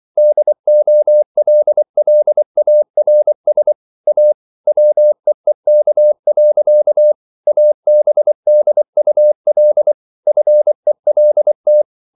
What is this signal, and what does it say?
Morse code, heard as a keyed tone, about 24 wpm, 600 Hz. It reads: DOLLARS A WEEK. ABDUL FELT